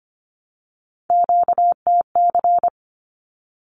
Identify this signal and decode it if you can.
Morse code, heard as a keyed tone, about 25 wpm, 700 Hz. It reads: QTC
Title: Q-Codes 25wpm